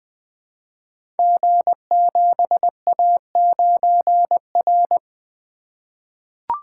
Morse code, heard as a keyed tone, about 20 wpm, 700 Hz. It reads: G7A9R
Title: Code Groups - 5 Letter:Numbers - 20wpm